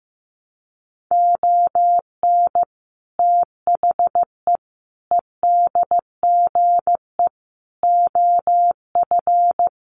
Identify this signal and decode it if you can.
Morse code, heard as a keyed tone, about 15 wpm, 700 Hz. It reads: ON THE EDGE OF
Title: Sets of 5 Words 15wpm